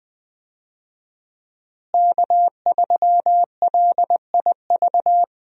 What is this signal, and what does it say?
Morse code, heard as a keyed tone, about 21 wpm, 700 Hz. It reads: K3LIV
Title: USA Call Signs - 5 Characters - Repeated 3x - 20wpm